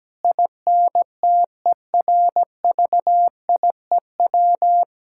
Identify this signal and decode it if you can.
Morse code, heard as a keyed tone, about 17 wpm, 700 Hz. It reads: INTERVIEW
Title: Sentences from Top 1600 Words 17wpm